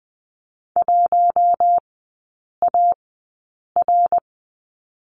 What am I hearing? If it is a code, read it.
Morse code, heard as a keyed tone, about 20 wpm, spaced as for 8 wpm, 700 Hz. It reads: 1AR
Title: 060 Lesson 11e - Letter h - Three Characters 20wpm